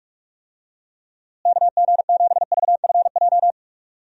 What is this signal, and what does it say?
Morse code, heard as a keyed tone, about 46 wpm, 700 Hz. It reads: KG7VFJ
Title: USA Call Signs - 6 Characters - Repeated 3x - 45wpm